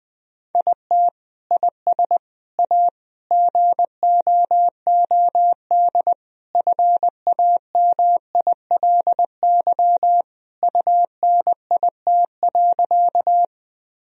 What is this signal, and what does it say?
Morse code, heard as a keyed tone, about 20 wpm, 700 Hz. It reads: IT IS A GOOD FAMILY UNIT.
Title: Sentences from Top 1000 Words - Review Part 2 - 20wpm